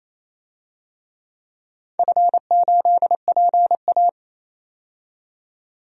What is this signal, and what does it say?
Morse code, heard as a keyed tone, about 27 wpm, 700 Hz. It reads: F8PA
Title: DXCC-100 Call Signs - 4 Characters - 28wpm